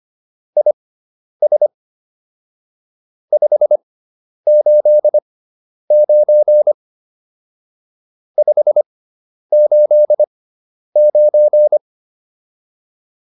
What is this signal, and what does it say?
Morse code, heard as a keyed tone, about 25 wpm, spaced as for 10 wpm, 600 Hz. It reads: IS 589 589
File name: QSO404 10